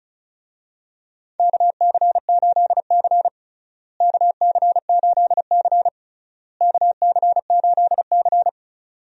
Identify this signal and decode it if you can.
Morse code, heard as a keyed tone, about 36 wpm, 700 Hz. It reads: KC8C KC8C KC8C
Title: USA Call Signs - 4 Characters - Repeated 3x - 35wpm